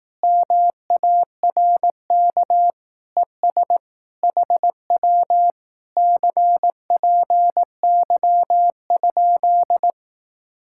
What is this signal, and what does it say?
Morse code, heard as a keyed tone, about 18 wpm, 700 Hz. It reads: MARK ES HW CPY?
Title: Speed-Racing Common QSO Phrases Encore - 15wpm